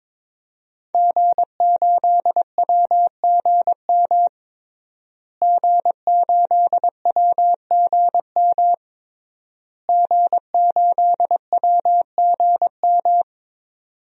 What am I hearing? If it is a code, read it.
Morse code, heard as a keyed tone, about 22 wpm, 700 Hz. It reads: G8WGM G8WGM G8WGM
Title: DXCC-100 Call Signs - 5 Characters - Repeated 3x - 22wpm